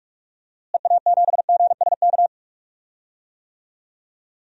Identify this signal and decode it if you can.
Morse code, heard as a keyed tone, about 46 wpm, 700 Hz. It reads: EA7GSK
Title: DXCC-100 Call Signs - 6 Characters - Repeated 3x - 45wpm